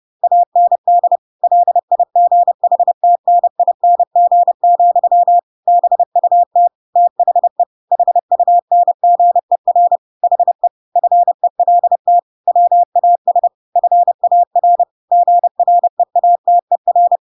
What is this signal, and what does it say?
Morse code, heard as a keyed tone, about 30 wpm, 700 Hz. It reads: AND LIGHTNING, BUT THE HUNGER HE FELT WAS FAR GREATER
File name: Pinocchio / Pinocchio ch6 30 30 ik1hge